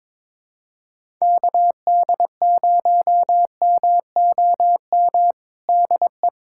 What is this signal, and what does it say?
Morse code, heard as a keyed tone, about 22 wpm, 700 Hz. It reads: KD0MOM DE
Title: USA Callsign DE Callsign - 6 Characters - 22wpm